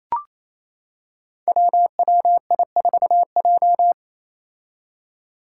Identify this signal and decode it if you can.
Morse code, heard as a keyed tone, about 28 wpm, 700 Hz. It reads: WWI4J
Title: Five Challenging Letter-Number Combinations 28wpm